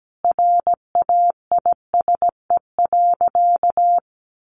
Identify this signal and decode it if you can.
Morse code, heard as a keyed tone, about 17 wpm, 700 Hz. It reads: RAISE.
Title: Sentences from Top 1400 Words 17wpm